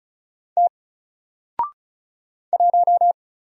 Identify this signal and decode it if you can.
Morse code, heard as a keyed tone, about 36 wpm, 700 Hz. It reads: T 1
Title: Single Letter-Number Alt 35wpm